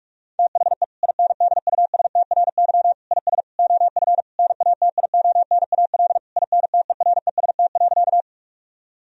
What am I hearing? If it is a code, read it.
Morse code, heard as a keyed tone, about 46 wpm, 700 Hz. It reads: THE INDUSTRY IS OF NATIONAL INTEREST.
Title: Sentences from Top 1000 Words - Review Part 4 - 45wpm